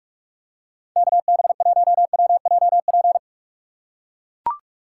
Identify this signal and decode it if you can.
Morse code, heard as a keyed tone, about 44 wpm, 700 Hz. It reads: KB1WJP
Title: USA Call Signs - 6 Characters - 6x Character Spacing - 45wpm